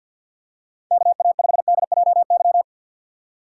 Morse code, heard as a keyed tone, about 53 wpm, 700 Hz. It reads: KA5DJY
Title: USA Call Signs - 6x Character Spacing - 50wpm